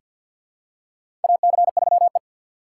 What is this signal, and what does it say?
Morse code, heard as a keyed tone, about 47 wpm, 700 Hz. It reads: AC3E
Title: USA Call Signs - 4 Characters - 6x Character Spacing - 50wpm